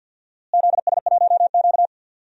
Morse code, heard as a keyed tone, about 52 wpm, 700 Hz. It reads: ZS1Q
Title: DXCC-100 Call Signs - 4 Characters - Repeated 2x - 50wpm